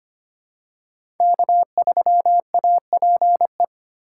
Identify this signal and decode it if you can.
Morse code, heard as a keyed tone, about 25 wpm, 700 Hz. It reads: K3APE
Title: USA Call Signs - 4x Character Spacing - 25wpm